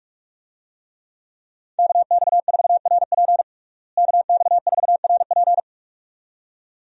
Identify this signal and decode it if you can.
Morse code, heard as a keyed tone, about 46 wpm, 700 Hz. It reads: KX4RP KX4RP
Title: USA Call Signs - 5 Characters - Repeated 2x - 45wpm